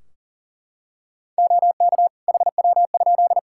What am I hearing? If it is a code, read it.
Morse code, heard as a keyed tone, about 40 wpm, 700 Hz. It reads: OK HW?
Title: K1USN SST QSO Practice 40wpm